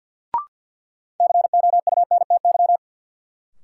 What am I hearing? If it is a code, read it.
Morse code, heard as a keyed tone, about 53 wpm, 700 Hz. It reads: COUNTY
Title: Sentences from Top 1200 Words - New Word Drill Alt - 50wpm